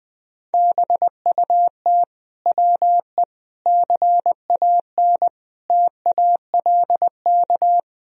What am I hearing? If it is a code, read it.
Morse code, heard as a keyed tone, about 20 wpm, 700 Hz. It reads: BUT WE CAN TALK
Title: Sentences from Top 1000 Words - Review Part 4 - 20wpm